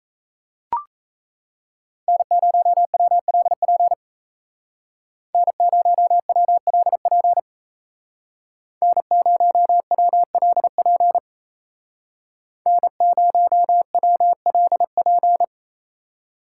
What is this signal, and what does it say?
Morse code, heard as a keyed tone, about 34 wpm, 700 Hz. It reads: N0WLP N0WLP N0WLP N0WLP
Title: Speed Racing - USA 1x3 Call Signs 28wpm